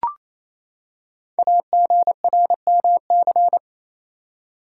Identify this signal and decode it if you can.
Morse code, heard as a keyed tone, about 27 wpm, 700 Hz. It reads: AGRMC
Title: Code Groups - 5 Letters - 28wpm